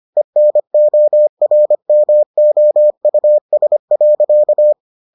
Morse code, heard as a keyed tone, about 25 wpm, 600 Hz. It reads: ENORMOUS.